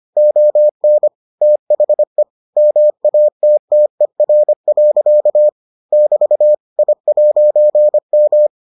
Morse code, heard as a keyed tone, about 25 wpm, 600 Hz. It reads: ON THE MATTER. <BT> I'M